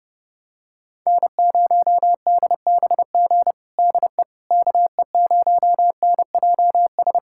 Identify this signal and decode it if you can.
Morse code, heard as a keyed tone, about 32 wpm, 700 Hz. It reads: N0DBG DE KE0NJS
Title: USA Callsign DE Callsign 30wpm